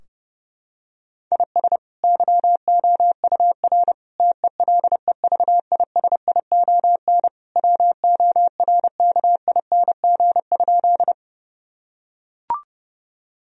Morse code, heard as a keyed tone, about 30 wpm, 700 Hz. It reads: IS YOUR TELEVISION WORKING?
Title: Sentences from Top 1000 Words - Review Part 4 - 30wpm